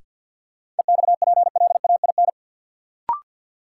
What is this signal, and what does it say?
Morse code, heard as a keyed tone, about 52 wpm, 700 Hz. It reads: EXPLAIN
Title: Sentences from Top 1100 Words - New Word Drill Alt - 50wpm